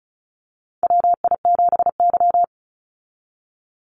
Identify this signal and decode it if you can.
Morse code, heard as a keyed tone, about 36 wpm, 700 Hz. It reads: WI7Y
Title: USA Call Signs 35wpm